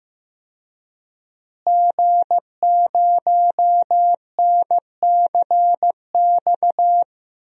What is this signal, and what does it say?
Morse code, heard as a keyed tone, about 15 wpm, 700 Hz. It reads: G0NCX
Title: DXCC-100 Call Signs - 5 Characters - Repeated 3x - 15wpm